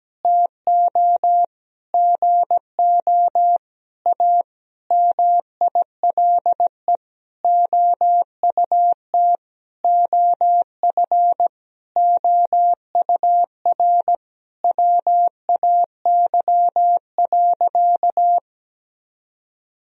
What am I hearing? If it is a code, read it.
Morse code, heard as a keyed tone, about 17 wpm, 700 Hz. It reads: TO GO A MILE OUT OF OUR WAY.
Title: Sentences from Top 1000 Words - Review Part 1 - 17wpm